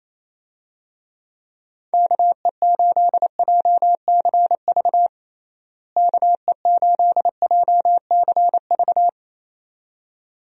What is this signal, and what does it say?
Morse code, heard as a keyed tone, about 28 wpm, 700 Hz. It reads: KE8JCV KE8JCV
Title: USA Call Signs - 6 Characters - Repeated 2x - 28wpm